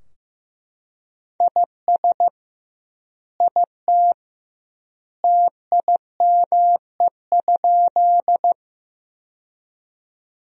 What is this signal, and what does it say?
Morse code, heard as a keyed tone, about 15 wpm, 700 Hz. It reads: IS IT TIME?
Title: 2x Word-Spacing - Sentences from Top 100 Words - 15wpm